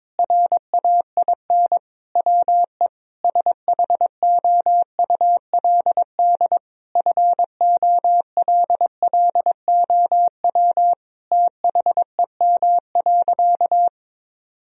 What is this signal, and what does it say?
Morse code, heard as a keyed tone, about 22 wpm, 700 Hz. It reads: RAIN WE SHOULD FOLLOW THEM.
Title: Sentences from Top 1000 Words - Review Part 1 - 22wpm